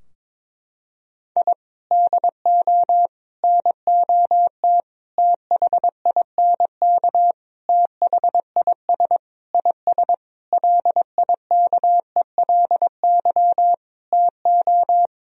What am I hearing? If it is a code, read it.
Morse code, heard as a keyed tone, about 22 wpm, 700 Hz. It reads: I DO NOT THINK THIS IS LIKELY TO
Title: Sentences from Top 1000 Words - Review Part 4 - 22wpm